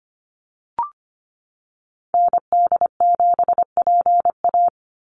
Morse code, heard as a keyed tone, about 25 wpm, 700 Hz. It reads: ND7PA
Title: NA SOTA Activator Callsigns 25wpm